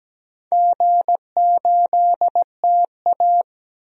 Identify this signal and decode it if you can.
Morse code, heard as a keyed tone, about 17 wpm, 700 Hz. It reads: G8TA
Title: DXCC-100 Call Signs - 4 Characters - Repeated 3x - 17wpm